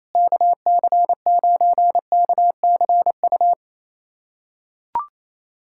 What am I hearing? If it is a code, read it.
Morse code, heard as a keyed tone, about 28 wpm, 700 Hz. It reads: KC9KCU
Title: USA Callsign DE Callsign 28wpm